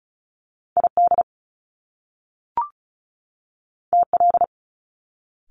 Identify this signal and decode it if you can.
Morse code, heard as a keyed tone, about 36 wpm, 700 Hz. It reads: ID TL
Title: Top Two Letter Combinations in Words 35wpm